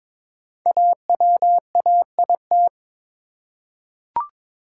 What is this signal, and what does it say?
Morse code, heard as a keyed tone, about 22 wpm, 700 Hz. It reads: AWAIT
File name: 5 Letter Words - First 3 Letters Given - 22wpm